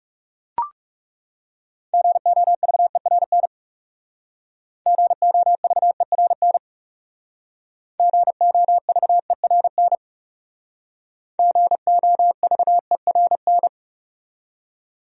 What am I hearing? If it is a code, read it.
Morse code, heard as a keyed tone, about 36 wpm, 700 Hz. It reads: GOVERN GOVERN GOVERN GOVERN
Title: Speed-Racing Top 500 Words 30wpm